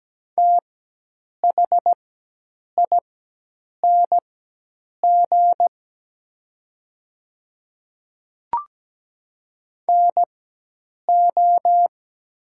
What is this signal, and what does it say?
Morse code, heard as a keyed tone, about 17 wpm, spaced as for 8 wpm, 700 Hz. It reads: THING NO
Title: 4x Character Spacing - Top 100 Words - 17wpm